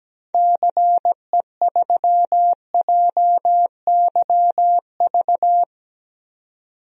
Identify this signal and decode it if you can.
Morse code, heard as a keyed tone, about 17 wpm, 700 Hz. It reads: CE3JYV
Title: DXCC-100 Call Signs - 6 Characters - 17wpm